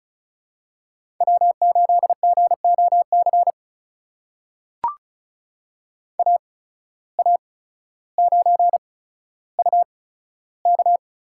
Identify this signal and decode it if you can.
Morse code, heard as a keyed tone, about 34 wpm, spaced as for 15 wpm, 700 Hz. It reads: W8GOC AA9UK
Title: USA Call Signs - 5 Characters - 8x Character Spacing - 35wpm